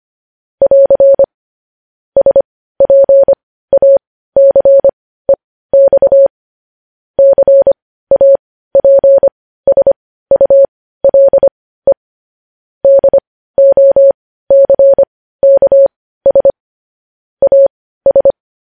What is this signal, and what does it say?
Morse code, heard as a keyed tone, about 25 wpm, spaced as for 15 wpm, 550 Hz. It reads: <AR> SPACEX CAPSULE DOCKS AS
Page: news headlines in morse code at 15 wpm